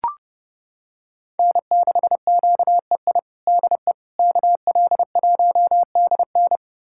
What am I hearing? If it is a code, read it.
Morse code, heard as a keyed tone, about 30 wpm, 700 Hz. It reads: N6QEI DE KL1DN
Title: USA Callsign DE Callsign - 5 Characters - 30wpm